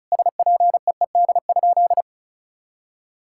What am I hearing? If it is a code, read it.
Morse code, heard as a keyed tone, about 36 wpm, 700 Hz. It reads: SPEED?
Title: Sentences from Top 1000 Words - Review Part 5 - 35wpm